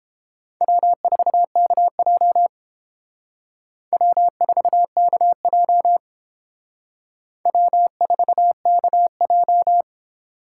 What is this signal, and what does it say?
Morse code, heard as a keyed tone, about 29 wpm, 700 Hz. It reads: W4KJ W4KJ W4KJ